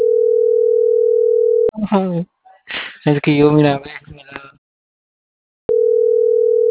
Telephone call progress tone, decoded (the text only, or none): ringback tone